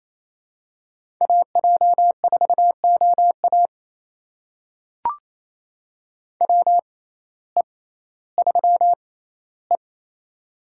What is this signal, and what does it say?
Morse code, heard as a keyed tone, about 28 wpm, spaced as for 15 wpm, 700 Hz. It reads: AJ4OA WE3E
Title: USA Call Signs - 6x Character Spacing - 28wpm